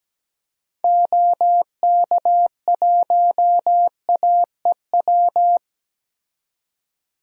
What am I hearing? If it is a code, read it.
Morse code, heard as a keyed tone, about 17 wpm, 700 Hz. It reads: OK1AEW
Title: DXCC-100 Call Signs - 6 Characters - 17wpm